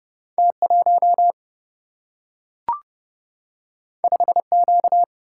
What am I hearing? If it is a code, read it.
Morse code, heard as a keyed tone, about 30 wpm, 700 Hz. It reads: T1 5Q
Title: Single-Letter Single-Number 30wpm